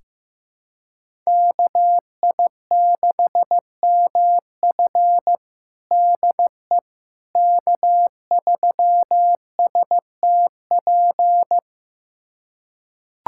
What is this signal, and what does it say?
Morse code, heard as a keyed tone, about 16 wpm, 700 Hz. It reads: KI6MF DE K3STP
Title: USA Callsign DE Callsign - 5 Characters - 15wpm